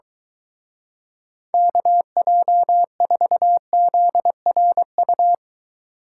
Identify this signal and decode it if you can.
Morse code, heard as a keyed tone, about 23 wpm, 700 Hz. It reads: KJ4ZRU